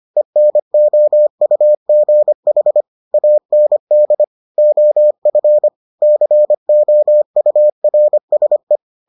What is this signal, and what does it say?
Morse code, heard as a keyed tone, about 25 wpm, 600 Hz. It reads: ENOUGH AND OF COURSE